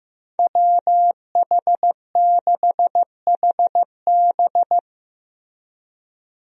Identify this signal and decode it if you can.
Morse code, heard as a keyed tone, about 15 wpm, 700 Hz. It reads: WH6HB
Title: USA Call Signs - 5 Characters - Repeated 3x - 15wpm